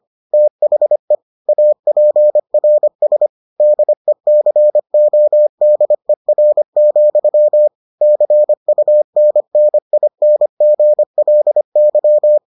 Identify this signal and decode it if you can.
Morse code, heard as a keyed tone, about 25 wpm, 600 Hz. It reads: THE APRS DECODER, CUNNINGLY